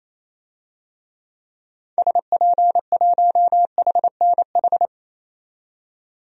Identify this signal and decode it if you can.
Morse code, heard as a keyed tone, about 28 wpm, 700 Hz. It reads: SP1HNH
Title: DXCC-100 Call Signs - 6 Characters - 28wpm